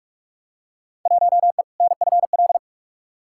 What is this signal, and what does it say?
Morse code, heard as a keyed tone, about 45 wpm, 700 Hz. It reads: 1E NFL
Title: Field Day Exchanges - 45wpm